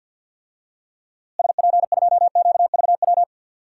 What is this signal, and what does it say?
Morse code, heard as a keyed tone, about 46 wpm, 700 Hz. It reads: SP2QVW
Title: DXCC-100 Call Signs - 6 Characters - Repeated 3x - 50wpm